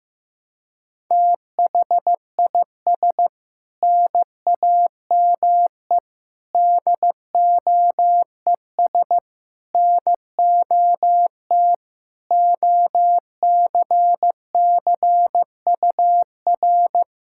Morse code, heard as a keyed tone, about 16 wpm, 700 Hz. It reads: THIS NAME DOES NOT OCCUR
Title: Sentences from Top 1600 Words 15wpm